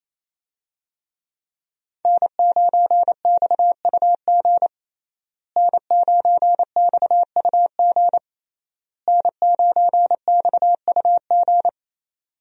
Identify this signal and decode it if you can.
Morse code, heard as a keyed tone, about 28 wpm, 700 Hz. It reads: N9XUG N9XUG N9XUG
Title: USA Call Signs - 5 Characters - Repeated 3x - 28wpm